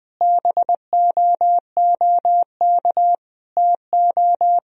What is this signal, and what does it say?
Morse code, heard as a keyed tone, about 19 wpm, 700 Hz. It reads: BOOK TO
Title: Sentences from Top 1000 Words - Review Part 1 - 20wpm